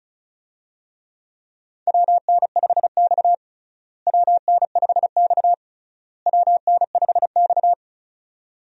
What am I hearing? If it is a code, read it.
Morse code, heard as a keyed tone, about 36 wpm, 700 Hz. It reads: WN5X WN5X WN5X
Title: USA Call Signs - 4 Characters - Repeated 3x - 35wpm